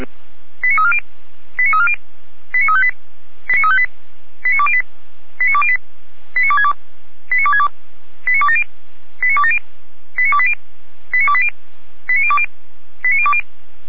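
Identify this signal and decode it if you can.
Five-tone selective call (ZVEI), heard as quick runs of five tones, tones of about 70 ms. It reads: 89340 89340 89358 89358 89208 89208 89262 89262 89270 89270 89280 89280 89020 89020
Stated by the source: ZVEI five-tone call 89319 received on SDR, logged by feuerwehr-orlamuende fire brigade.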